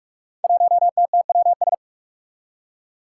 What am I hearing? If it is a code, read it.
Morse code, heard as a keyed tone, about 43 wpm, 700 Hz. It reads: 1TTWS
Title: Five Challenging Letter-Number Combinations 45wpm